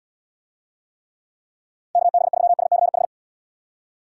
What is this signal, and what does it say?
Morse code, heard as a keyed tone, about 71 wpm, 700 Hz. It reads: KB3SCH